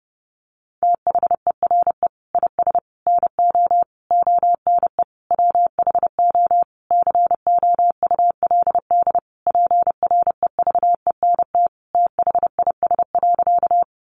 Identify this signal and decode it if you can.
Morse code, heard as a keyed tone, about 30 wpm, 700 Hz. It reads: THERE IS NO ONE WHO COULD PREVENT THIS.
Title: Sentences from Top 1100 Words 30wpm